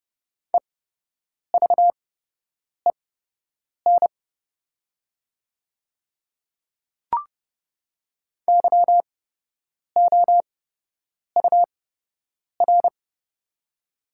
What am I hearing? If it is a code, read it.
Morse code, heard as a keyed tone, about 30 wpm, spaced as for 8 wpm, 700 Hz. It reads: EVEN YOUR
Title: 8x Character Spacing - Top 200 Words - 30wpm